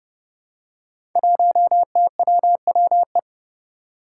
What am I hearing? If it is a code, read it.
Morse code, heard as a keyed tone, about 29 wpm, 700 Hz. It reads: 1TWWE
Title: Five Challenging Letter-Number Combinations 30wpm